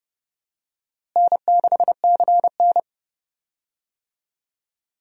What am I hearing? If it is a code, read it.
Morse code, heard as a keyed tone, about 31 wpm, 700 Hz. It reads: N6CN
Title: USA Call Signs - 4 Characters - Repeated 3x - 30wpm